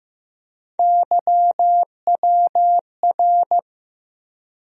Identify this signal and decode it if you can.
Morse code, heard as a keyed tone, about 15 wpm, 700 Hz. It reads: YWR
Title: USA Call Sign Postfixes - 3 Letters 15wpm